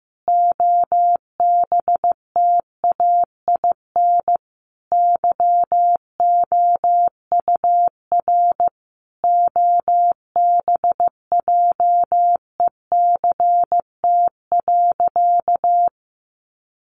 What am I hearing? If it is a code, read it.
Morse code, heard as a keyed tone, about 15 wpm, 700 Hz. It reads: OBTAIN YOUR OBJECT.
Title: Sentences from Top 1300 Words 15wpm